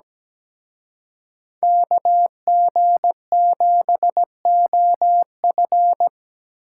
Morse code, heard as a keyed tone, about 20 wpm, 700 Hz. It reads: KG7OF
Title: Speed Racing - USA 2x2 Call Signs 17wpm